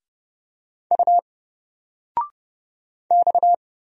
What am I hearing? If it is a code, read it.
Morse code, heard as a keyed tone, about 30 wpm, 700 Hz. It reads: U X
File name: Single-Letter-Number-30wpm